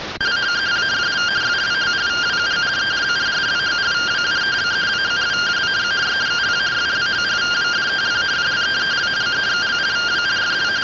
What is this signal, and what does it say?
Two-tone teleprinter signal, about 170 Hz shift.